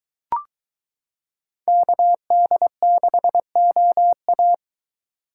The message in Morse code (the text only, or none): KD6OA